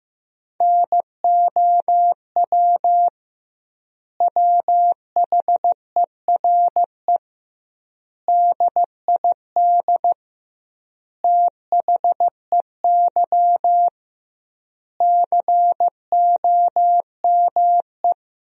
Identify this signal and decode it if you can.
Morse code, heard as a keyed tone, about 15 wpm, 700 Hz. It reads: NOW WHERE DID THEY COME
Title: 2x Word-Spacing - Sentences from Top 200 Words - 15wpm